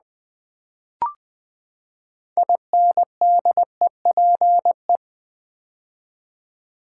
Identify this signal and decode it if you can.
Morse code, heard as a keyed tone, about 20 wpm, 700 Hz. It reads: INDEPE
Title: Build A Word - 6 Letters At A Time - 12-Letter Words - 20wpm